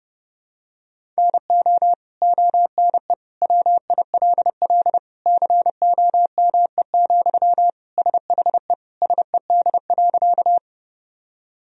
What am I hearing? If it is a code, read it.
Morse code, heard as a keyed tone, about 30 wpm, 700 Hz. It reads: NO ONE WILL COME, SHE SED.
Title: Sentences from Top 100 Words and Abbreviations 30wpm